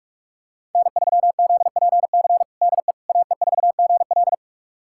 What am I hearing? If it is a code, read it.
Morse code, heard as a keyed tone, about 46 wpm, 700 Hz. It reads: N3ZPC DE AE4GL
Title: USA Callsign DE Callsign - 5 Characters - 45wpm